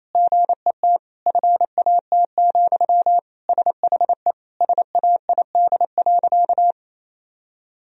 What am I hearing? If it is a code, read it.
Morse code, heard as a keyed tone, about 28 wpm, 700 Hz. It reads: GET FAT, SHE SAID.
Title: Sentences from Top 1000 Words - Review Part 4 - 28wpm